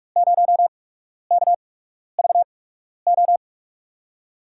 Morse code, heard as a keyed tone, about 46 wpm, spaced as for 12 wpm, 700 Hz. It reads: 0KVO
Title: USA Call Signs - 6 Characters - 8x Character Spacing - 45wpm